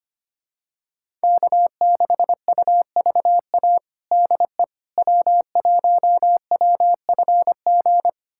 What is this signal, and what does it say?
Morse code, heard as a keyed tone, about 25 wpm, 700 Hz. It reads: K6UVA DE W1WFG
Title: USA Callsign DE Callsign - 5 Characters - 25wpm